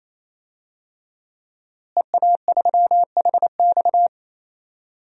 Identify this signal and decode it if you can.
Morse code, heard as a keyed tone, about 28 wpm, 700 Hz. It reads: EA3HX
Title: DXCC-100 Call Signs - 5 Characters - Repeated 3x - 28wpm